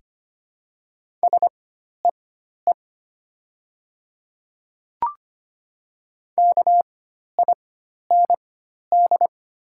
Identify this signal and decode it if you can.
Morse code, heard as a keyed tone, about 25 wpm, spaced as for 12 wpm, 690 Hz. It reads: SEE KIND
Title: 4x Character Spacing - Top 500 Words - 25wpm